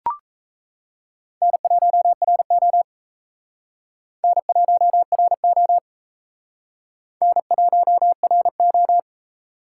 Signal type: Morse code

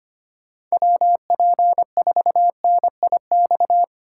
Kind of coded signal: Morse code

